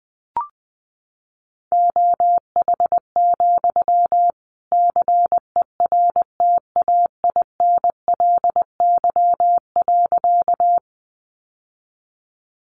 Morse code, heard as a keyed tone, about 20 wpm, 700 Hz. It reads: OH, CERTAINLY.
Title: Sentences from Top 900 Words 20wpm